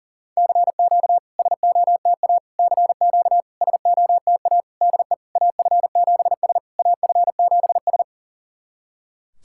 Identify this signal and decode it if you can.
Morse code, heard as a keyed tone, about 40 wpm, 690 Hz. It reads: CQ SOTA CQ SOTA DE AF7S AF7S
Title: SOTA QSO Practice 40wpm